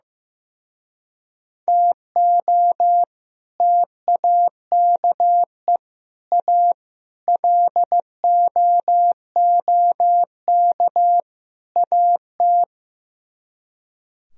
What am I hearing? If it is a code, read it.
Morse code, heard as a keyed tone, about 15 wpm, 700 Hz. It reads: TO TAKE A LOOK AT
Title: Speed-Racing - Sets of 5 Words 15wpm